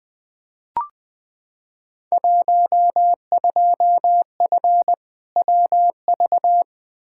Morse code, heard as a keyed tone, about 20 wpm, 700 Hz. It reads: 12F WV
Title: Field Day Exchanges Encore - 20wpm